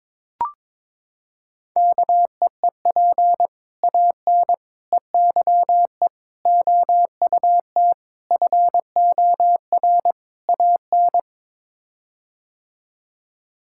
Morse code, heard as a keyed tone, about 22 wpm, 700 Hz. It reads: KEEP AN EYE OUT FOR AN
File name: practice / 22wpm / Sets of 6 Words - 22wpm